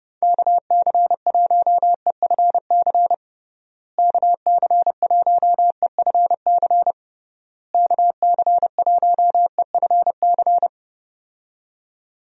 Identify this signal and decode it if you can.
Morse code, heard as a keyed tone, about 32 wpm, 700 Hz. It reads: KC1EFC KC1EFC KC1EFC
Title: USA Call Signs - 6 Characters - Repeated 3x - 30wpm